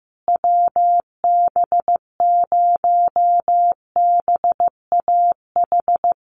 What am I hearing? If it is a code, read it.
Morse code, heard as a keyed tone, about 15 wpm, 700 Hz. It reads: WB0BAH
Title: USA Call Signs Repeated 3x 15wpm